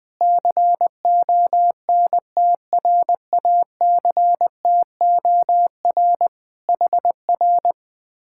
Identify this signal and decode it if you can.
Morse code, heard as a keyed tone, about 20 wpm, 700 Hz. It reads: CONTRACTOR HR
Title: Speed-Racing Common QSO Phrases - 20wpm